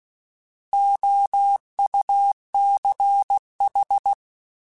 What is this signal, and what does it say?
Morse code, heard as a keyed tone, about 16 wpm, 800 Hz. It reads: OUCH
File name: Alphabetical / Alphabetical mp3s / Os @15